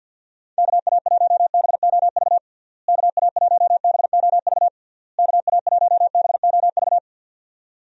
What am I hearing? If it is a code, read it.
Morse code, heard as a keyed tone, about 49 wpm, 700 Hz. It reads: KA1BOV KA1BOV KA1BOV
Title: USA Call Signs - 6 Characters - Repeated 3x - 50wpm